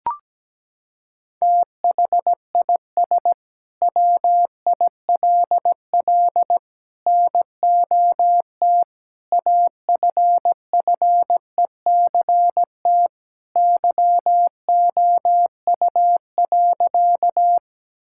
Morse code, heard as a keyed tone, about 17 wpm, 700 Hz. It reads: THIS WILL NOT AFFECT YOU.
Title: Sentences from Top 1000 Words - Review Part 6 - 17wpm